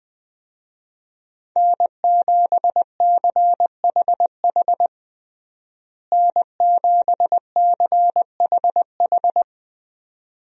Morse code, heard as a keyed tone, about 21 wpm, 700 Hz. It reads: N7CHH N7CHH
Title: USA Call Signs - 5 Characters - Repeated 2x - 20wpm